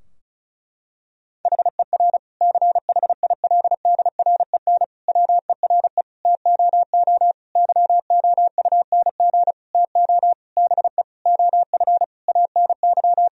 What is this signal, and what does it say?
Morse code, heard as a keyed tone, about 36 wpm, 700 Hz. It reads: HER CHILDREN WERE TOO YOUNG TO BE OF ANY
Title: Sentences from Top 1000 Words - Review Part 4 - 35wpm